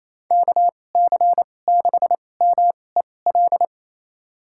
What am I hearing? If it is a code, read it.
Morse code, heard as a keyed tone, about 27 wpm, spaced as for 20 wpm, 700 Hz. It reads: KC6MEL
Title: USA Call Signs - 6 Characters - 2x Character Spacing - 28wpm